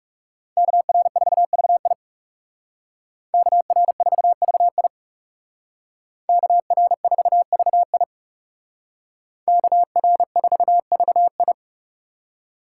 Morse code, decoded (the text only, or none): KR4VI KR4VI KR4VI KR4VI